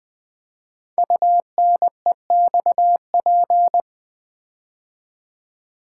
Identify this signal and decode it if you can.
Morse code, heard as a keyed tone, about 20 wpm, 700 Hz. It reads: UNEXP
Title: Build A Word - 5 Letters At A Time - 10-Letter Words - 20wpm